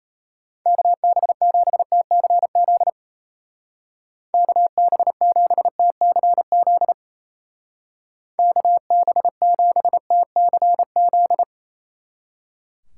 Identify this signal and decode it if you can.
Morse code, heard as a keyed tone, about 34 wpm, 700 Hz. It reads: KB7TCZ KB7TCZ KB7TCZ
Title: Speed Racing - USA 2x3 Call Signs 28wpm